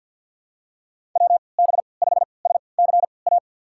Morse code, heard as a keyed tone, about 52 wpm, spaced as for 28 wpm, 700 Hz. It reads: WB5SCA
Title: USA Call Signs - 6 Characters - 3x Character Spacing - 50wpm